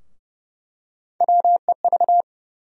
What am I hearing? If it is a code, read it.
Morse code, heard as a keyed tone, about 30 wpm, 700 Hz. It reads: WEV
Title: USA Call Sign Postfixes - 3 Letters 30wpm